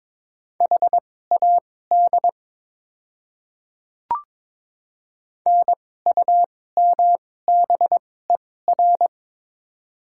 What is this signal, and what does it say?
Morse code, heard as a keyed tone, about 21 wpm, spaced as for 16 wpm, 700 Hz. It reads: HAD NUMBER
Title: 2x Character Spacing - Top 100 Words - 22wpm